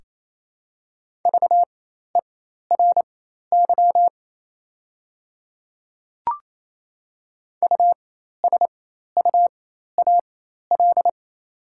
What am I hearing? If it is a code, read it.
Morse code, heard as a keyed tone, about 28 wpm, spaced as for 13 wpm, 690 Hz. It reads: VERY USUAL
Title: 4x Character Spacing - Top 500 Words - 28wpm